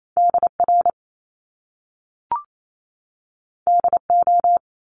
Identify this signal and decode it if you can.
Morse code, heard as a keyed tone, about 28 wpm, 700 Hz. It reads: DR DO
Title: Top Two Letter Combinations in Words 28wpm